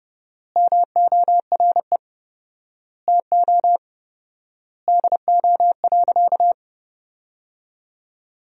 Morse code, heard as a keyed tone, about 30 wpm, 700 Hz. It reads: MORE TO DO.
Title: 4x Word-Spacing - Sentences from Top 100 Words - 30wpm